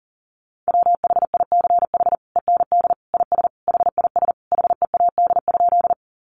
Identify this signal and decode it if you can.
Morse code, heard as a keyed tone, about 40 wpm, 700 Hz. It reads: WHICH END IS HIS HEAD?
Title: Sentences from Top 300 Words 40wpm